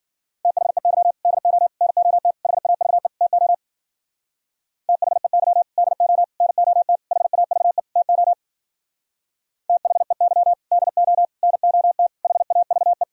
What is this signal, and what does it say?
Morse code, heard as a keyed tone, about 52 wpm, 700 Hz. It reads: THEY DO NOT HAVE TO THEY DO NOT HAVE TO THEY DO NOT HAVE